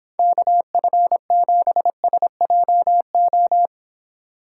Morse code, heard as a keyed tone, about 25 wpm, 700 Hz. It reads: KF7SJO